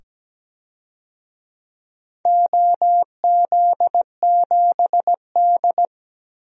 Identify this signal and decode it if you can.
Morse code, heard as a keyed tone, about 17 wpm, 700 Hz. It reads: OZ7D
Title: DXCC-100 Call Signs - 4 Characters - 17wpm